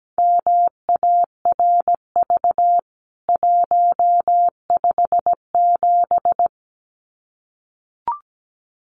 Morse code, heard as a keyed tone, about 17 wpm, 700 Hz. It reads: MARV 157
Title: CWT Exchange 17wpm